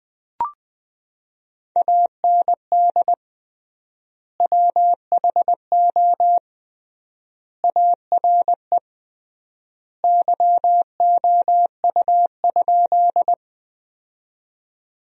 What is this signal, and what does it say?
Morse code, heard as a keyed tone, about 21 wpm, 700 Hz. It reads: AND WHO ARE YOU?
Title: 3x Word-Spacing - Sentences from Top 100 Words - 20wpm